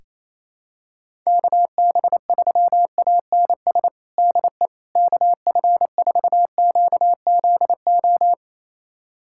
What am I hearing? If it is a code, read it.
Morse code, heard as a keyed tone, about 28 wpm, 700 Hz. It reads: KB3ANS DE KF4QZO
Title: USA Callsign DE Callsign - 6 Characters - 28wpm